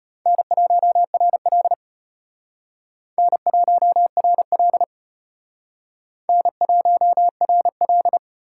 Morse code, heard as a keyed tone, about 32 wpm, 700 Hz. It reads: N1RL N1RL N1RL